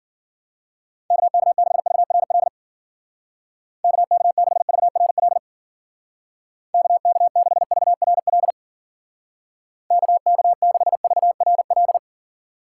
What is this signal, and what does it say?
Morse code, heard as a keyed tone, about 52 wpm, 700 Hz. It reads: KK6VRL KK6VRL KK6VRL KK6VRL